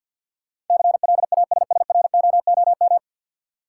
Morse code, heard as a keyed tone, about 47 wpm, 700 Hz. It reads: CLASSROOM
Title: Sentences from Top 1500 Words - New Word Drill - 50wpm